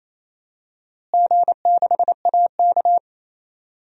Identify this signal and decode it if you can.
Morse code, heard as a keyed tone, about 27 wpm, 700 Hz. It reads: G6AK
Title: DXCC-100 Call Signs - 4 Characters - Repeated 2x - 28wpm